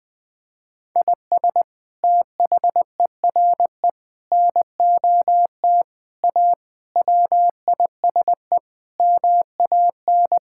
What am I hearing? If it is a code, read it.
Morse code, heard as a keyed tone, about 21 wpm, 700 Hz. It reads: IS THERE NOT A WISE MAN
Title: Sentences from Top 1600 Words 20wpm